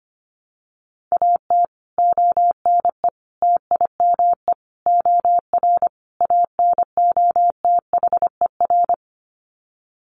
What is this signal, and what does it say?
Morse code, heard as a keyed tone, about 25 wpm, 700 Hz. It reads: AT ONE TIME OR ANOTHER
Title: Sets of 5 Words Encore 25wpm